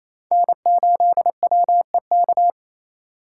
Morse code, heard as a keyed tone, about 28 wpm, 700 Hz. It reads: N8WEK
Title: USA Call Signs - 2x Character Spacing - 28wpm